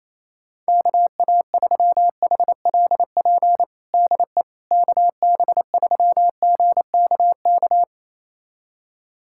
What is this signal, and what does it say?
Morse code, heard as a keyed tone, about 27 wpm, 700 Hz. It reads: KA3HLP DE KB3GKK
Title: USA Callsign DE Callsign 28wpm